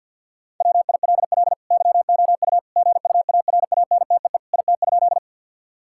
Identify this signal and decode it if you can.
Morse code, heard as a keyed tone, about 50 wpm, 700 Hz. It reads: WILL YOU GUARANTEE IT?
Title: Sentences from Top 1600 Words 50wpm